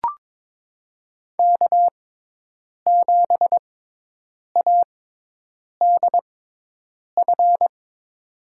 Morse code, heard as a keyed tone, about 22 wpm, spaced as for 8 wpm, 700 Hz. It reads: K7ADF
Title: USA Call Signs - 6x Character Spacing - 22wpm